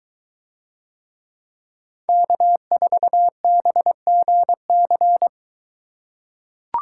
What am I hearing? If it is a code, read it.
Morse code, heard as a keyed tone, about 22 wpm, 700 Hz. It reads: K4BGC